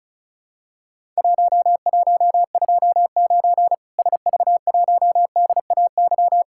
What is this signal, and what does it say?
Morse code, heard as a keyed tone, about 36 wpm, 700 Hz. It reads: 1129 SV1DAY
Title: Worked All Europe DX Contest - QTC Exchange - 35wpm